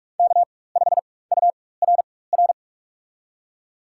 Morse code, heard as a keyed tone, about 45 wpm, spaced as for 21 wpm, 700 Hz. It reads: K5URR
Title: USA Call Signs - 5 Characters - 4x Character Spacing - 45wpm